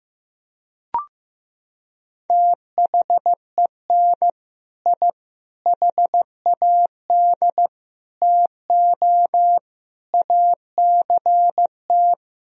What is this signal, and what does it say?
Morse code, heard as a keyed tone, about 16 wpm, 700 Hz. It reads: THEN I HAD TO ACT
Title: Sentences from Top 1000 Words - Review Part 1 - 15wpm